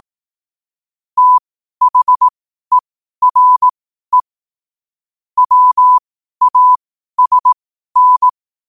Morse code, heard as a keyed tone, about 18 wpm, spaced as for 12 wpm, 1000 Hz. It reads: THERE WASN
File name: qotdfiles / 12wpm